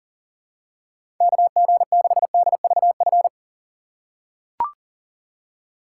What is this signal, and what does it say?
Morse code, heard as a keyed tone, about 40 wpm, 700 Hz. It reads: KG6DVF
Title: USA Call Signs - 6 Characters - 4x Character Spacing - 40wpm